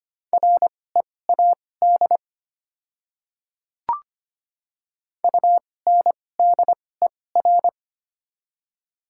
Morse code, heard as a keyed tone, about 25 wpm, spaced as for 18 wpm, 700 Hz. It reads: READ UNDER
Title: 2x Character Spacing - Top 200 Words - 25wpm